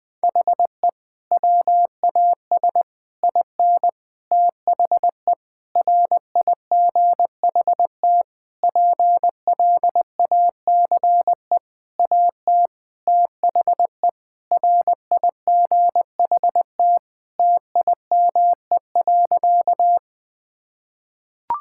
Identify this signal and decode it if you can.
Morse code, heard as a keyed tone, about 19 wpm, 700 Hz. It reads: HE WAS IN THE RIGHT PLACE AT THE RIGHT TIME.
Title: Sentences with 6-Word Phrases - 20wpm